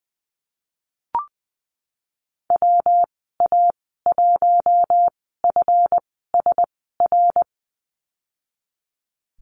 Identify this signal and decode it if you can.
Morse code, heard as a keyed tone, about 20 wpm, spaced as for 15 wpm, 700 Hz. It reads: WA1FSR
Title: 147 Lesson 24g - Letter y - Call Signs 20wpm